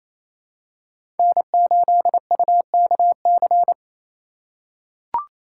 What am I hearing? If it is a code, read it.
Morse code, heard as a keyed tone, about 28 wpm, 700 Hz. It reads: N8UKC
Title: USA Call Signs - 4x Character Spacing - 28wpm